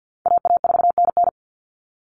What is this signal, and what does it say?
Morse code, heard as a keyed tone, about 48 wpm, 700 Hz. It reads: AA4NN
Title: Call Signs Repeated 3x - 50wpm